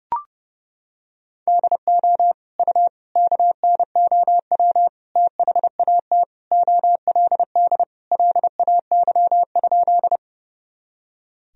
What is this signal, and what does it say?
Morse code, heard as a keyed tone, about 30 wpm, 700 Hz. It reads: DO U KNOW THAT OLD LAY?
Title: Sentences from Top 400 Words and Abbreviations 30wpm